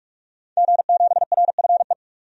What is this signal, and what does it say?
Morse code, heard as a keyed tone, about 41 wpm, 700 Hz. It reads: G7RFE
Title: DXCC-100 Call Signs - 5 Characters - 45wpm